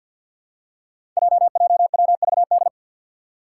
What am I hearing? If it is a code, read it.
Morse code, heard as a keyed tone, about 52 wpm, 700 Hz. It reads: JJWVD
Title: Code Groups - 5 Letters - 50wpm